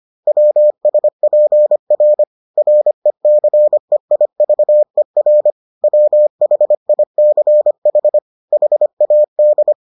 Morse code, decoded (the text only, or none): WSPR RECEIVER WHICH HAD